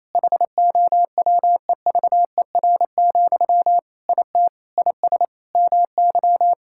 Morse code, heard as a keyed tone, about 28 wpm, 700 Hz. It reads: HOWEVER, IT IS MY
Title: Sentences from Top 1600 Words 28wpm